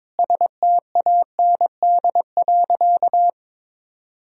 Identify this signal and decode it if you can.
Morse code, heard as a keyed tone, about 22 wpm, 700 Hz. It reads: STAND.
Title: Sentences from Top 1000 Words - Review Part 1 - 22wpm